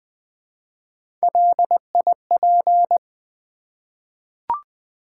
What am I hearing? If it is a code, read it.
Morse code, heard as a keyed tone, about 20 wpm, 700 Hz. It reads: LIP
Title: Three Letter Words - 20wpm